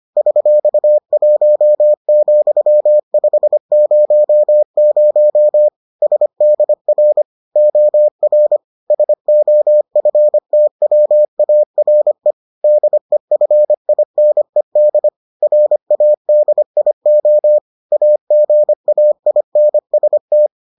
Morse code, heard as a keyed tone, about 25 wpm, 600 Hz. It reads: $1,500 SDR OR SOFTWARE DEFINED RADIO AGAINST